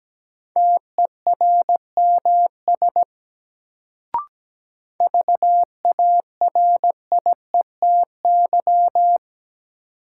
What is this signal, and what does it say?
Morse code, heard as a keyed tone, about 17 wpm, 700 Hz. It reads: TERMS VARIETY
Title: Sets of 3 Words - New Word Drill - 17wpm